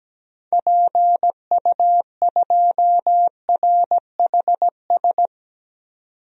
Morse code, heard as a keyed tone, about 17 wpm, 700 Hz. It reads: PU2RHS
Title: DXCC-100 Call Signs - 6 Characters - 17wpm